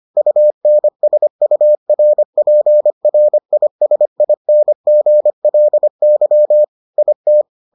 Morse code, heard as a keyed tone, about 25 wpm, 600 Hz. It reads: UNSURPRISINGLY IT